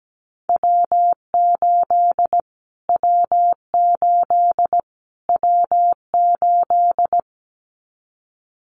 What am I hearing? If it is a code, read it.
Morse code, heard as a keyed tone, about 17 wpm, 700 Hz. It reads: W8 W8 W8
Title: Callsign Prefixes for Top 25 DXCC - Repeated 3x - 17wpm